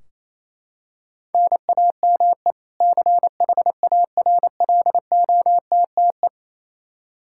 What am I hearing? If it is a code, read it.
Morse code, heard as a keyed tone, about 28 wpm, 700 Hz. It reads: NAME CHARLOTTE
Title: Practice QSO - Exchange 1 - Beginning - 28wpm